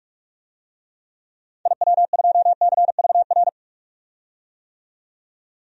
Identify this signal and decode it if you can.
Morse code, heard as a keyed tone, about 42 wpm, 700 Hz. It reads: IW2CVR
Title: DXCC-100 Call Signs - 6 Characters - 45wpm